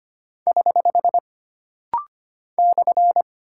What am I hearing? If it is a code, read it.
Morse code, heard as a keyed tone, about 25 wpm, 700 Hz. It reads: <HH> /
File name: Single Punctuation-Prosign - Rapid-Fire - 25wpm